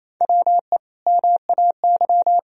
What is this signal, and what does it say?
Morse code, heard as a keyed tone, about 28 wpm, 700 Hz. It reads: WE MAY
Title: Sentences from Top 1000 Words - Review Part 1 - 28wpm